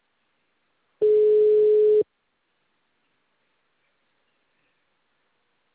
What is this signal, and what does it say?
Ringback tone (Europe, 425 Hz).
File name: out- -2004- - - .2135